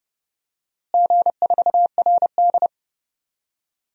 Morse code, heard as a keyed tone, about 30 wpm, 700 Hz. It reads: G4RD
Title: DXCC-100 Call Signs - 4 Characters - Repeated 3x - 30wpm